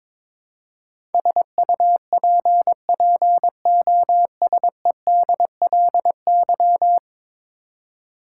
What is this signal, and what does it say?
Morse code, heard as a keyed tone, about 22 wpm, 700 Hz. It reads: SUPPOSEDLY
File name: practice / 22wpm / 10 Letter Words - First 3 Letters Given - 22wpm